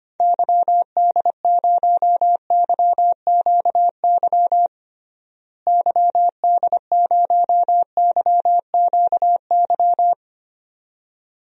Morse code, heard as a keyed tone, about 25 wpm, 700 Hz. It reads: YD0YQY YD0YQY
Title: DXCC-100 Call Signs - 6 Characters - Repeated 2x - 25wpm